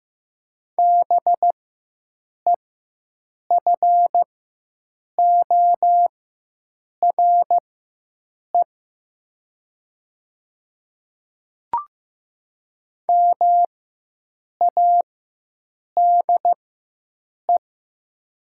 Morse code, heard as a keyed tone, about 15 wpm, spaced as for 7 wpm, 700 Hz. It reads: BEFORE MADE
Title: 4x Character Spacing - Top 300 Words - 15wpm